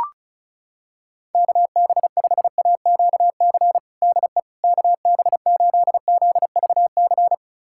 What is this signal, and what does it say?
Morse code, heard as a keyed tone, about 36 wpm, 700 Hz. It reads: KB5AQC DE KB8ZVC
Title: USA Callsign DE Callsign - 6 Characters - 35wpm